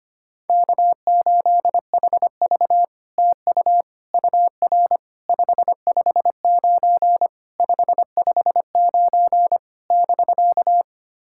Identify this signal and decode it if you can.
Morse code, heard as a keyed tone, about 25 wpm, 700 Hz. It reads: K8HV TU UR 559 559 <BK>
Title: SOTA QSO Practice 25wpm